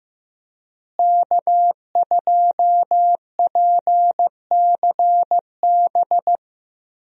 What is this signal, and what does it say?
Morse code, heard as a keyed tone, about 15 wpm, 700 Hz. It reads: K2PCB
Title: USA Call Signs - 5 Characters - Repeated 2x - 15wpm